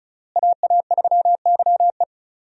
Morse code, heard as a keyed tone, about 34 wpm, 700 Hz. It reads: AA3YE
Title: USA Call Signs - 5 Characters - Repeated 3x - 35wpm